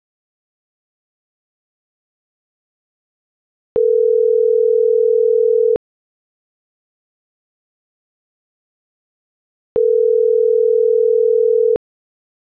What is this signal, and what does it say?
Ringback tone (North America), 440 + 480 Hz.
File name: CENTRAL MAYAB / out dulce.silva